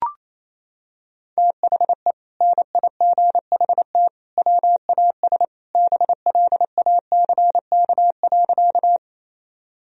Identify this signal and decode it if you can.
Morse code, heard as a keyed tone, about 28 wpm, 700 Hz. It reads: THE NIGHT WAS BLACK.
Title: Sentences from Top 1000 Words - Review Part 2 - 28wpm